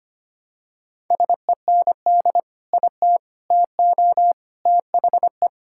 Morse code, heard as a keyed tone, about 25 wpm, 700 Hz. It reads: SEND IT TO THE
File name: practice / 25wpm / Sentences with 3-Word Phrases - 25wpm